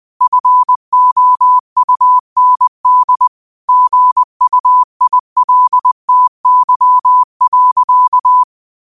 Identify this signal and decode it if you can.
Morse code, heard as a keyed tone, about 20 wpm, 1000 Hz. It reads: FOUND GUILTY.
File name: 20wpm